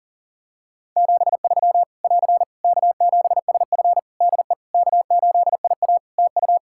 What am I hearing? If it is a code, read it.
Morse code, heard as a keyed tone, about 40 wpm, 700 Hz. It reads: 73 <AR> K7SF DE K8IA TU
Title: Practice QSO - Exchange 4 - Ending - 40wpm